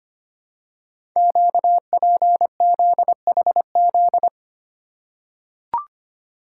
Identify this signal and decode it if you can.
Morse code, heard as a keyed tone, about 25 wpm, 700 Hz. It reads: QPZHZ
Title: Code Groups - 5 Letters - 25wpm